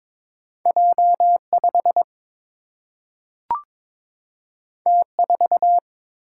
Morse code, heard as a keyed tone, about 22 wpm, 700 Hz. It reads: J 5 T 4
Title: Single-Letter Single-Number 22wpm